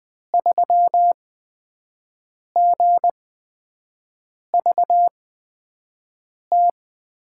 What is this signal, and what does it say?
Morse code, heard as a keyed tone, about 20 wpm, spaced as for 5 wpm, 700 Hz. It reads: 3GVT
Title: USA Call Signs - 8x Character Spacing - 20wpm